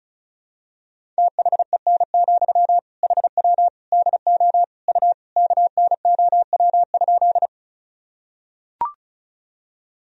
Morse code, heard as a keyed tone, about 36 wpm, 700 Hz. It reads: THEN, HW DO U KNOW?
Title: Sentences from Top 100 Words and Abbreviations 35wpm